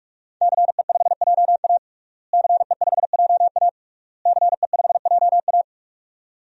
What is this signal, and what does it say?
Morse code, heard as a keyed tone, about 46 wpm, 700 Hz. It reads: CE5JA CE5JA CE5JA
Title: DXCC-100 Call Signs - 5 Characters - Repeated 3x - 45wpm